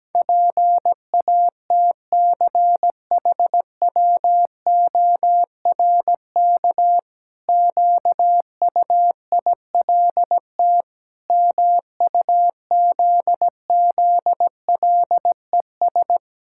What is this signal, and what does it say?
Morse code, heard as a keyed tone, about 17 wpm, 700 Hz. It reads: PATCHWORK QUILT MUZZLES
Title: Pangram Challenge v2 - 17wpm